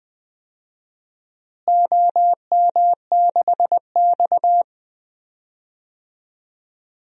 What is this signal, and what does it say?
Morse code, heard as a keyed tone, about 20 wpm, 700 Hz. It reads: OM6X
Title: DXCC-100 Call Signs - 4 Characters - Repeated 2x - 20wpm